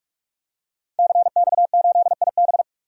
Morse code, heard as a keyed tone, about 46 wpm, 700 Hz. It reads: CX8IB
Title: DXCC-100 Call Signs - 5 Characters - 45wpm